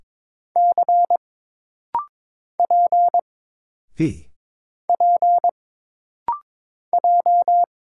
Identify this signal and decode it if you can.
Morse code, heard as a keyed tone, about 22 wpm, 700 Hz. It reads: C P P J
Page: This rapid-fire timing is geared to push your ICR (Instant Character Recognition) skills.